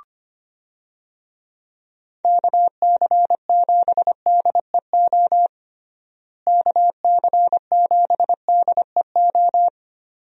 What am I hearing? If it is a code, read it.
Morse code, heard as a keyed tone, about 25 wpm, 700 Hz. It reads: KC7DEO KC7DEO
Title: USA Call Signs - 6 Characters - Repeated 2x - 25wpm